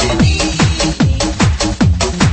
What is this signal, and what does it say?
There is no code.